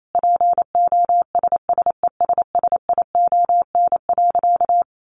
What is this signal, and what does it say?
Morse code, heard as a keyed tone, about 28 wpm, 700 Hz. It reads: POSSESSION.
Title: Sentences from Top 1400 Words 28wpm